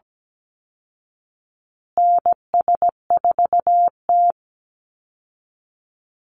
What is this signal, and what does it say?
Morse code, heard as a keyed tone, about 17 wpm, 700 Hz. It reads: NS4T
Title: Call Signs Repeated 3x - 17wpm